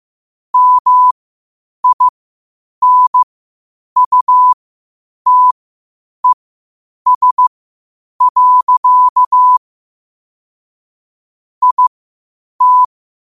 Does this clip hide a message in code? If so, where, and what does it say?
Yes: Morse code from 0.5 to 12.8 s, heard as a keyed tone, about 15 wpm, spaced as for 8 wpm, 1000 Hz. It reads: MINUTES. IT